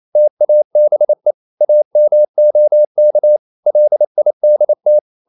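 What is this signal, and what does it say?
Morse code, heard as a keyed tone, about 28 wpm, 600 Hz. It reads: TABE AMOK LIDT
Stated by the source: Danish short words sent in Morse at 28 wpm, from CW med Gnister.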